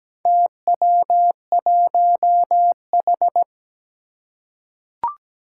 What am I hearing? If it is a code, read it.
Morse code, heard as a keyed tone, about 17 wpm, 700 Hz. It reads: TW1H
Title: Four Challenging Letter-Number Combinations 17wpm